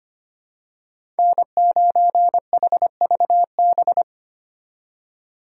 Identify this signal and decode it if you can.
Morse code, heard as a keyed tone, about 25 wpm, 700 Hz. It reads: N9HVB
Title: USA Call Signs - 3x Character Spacing - 25wpm